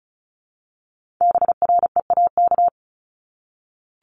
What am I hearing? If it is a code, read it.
Morse code, heard as a keyed tone, about 36 wpm, 700 Hz. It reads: BREAK
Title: Multi-letter Phonemes 35wpm